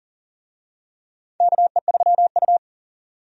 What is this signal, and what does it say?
Morse code, heard as a keyed tone, about 40 wpm, 700 Hz. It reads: KE3U
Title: USA Call Signs - 4 Characters - 40wpm